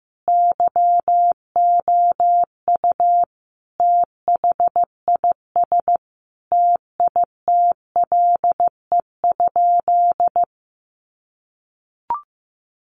Morse code, heard as a keyed tone, about 15 wpm, 700 Hz. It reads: YOU THIS TITLE?
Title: Sentences from Top 1200 Words 15wpm